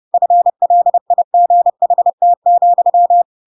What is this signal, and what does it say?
Morse code, heard as a keyed tone, about 30 wpm, 700 Hz. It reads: FLIGHT,